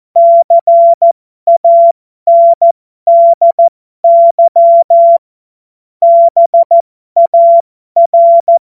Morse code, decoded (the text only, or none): CANDY BAR